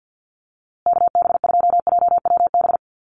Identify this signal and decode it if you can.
Morse code, heard as a keyed tone, about 50 wpm, 700 Hz. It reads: KB2JWB
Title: USA Call Signs Repeated 3x 50wpm